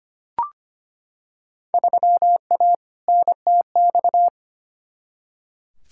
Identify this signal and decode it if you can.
Morse code, heard as a keyed tone, about 25 wpm, 700 Hz. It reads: 3A NTX
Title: Field Day Exchanges - 25wpm